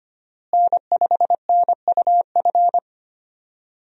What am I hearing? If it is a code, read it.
Morse code, heard as a keyed tone, about 25 wpm, 700 Hz. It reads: N5NUF
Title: USA Call Signs - 5 Characters - Repeated 3x - 25wpm